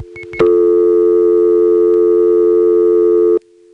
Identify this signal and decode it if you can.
Dial tone (North America), 350 + 440 Hz.
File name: phonetrips / SmyrnaGA5HQ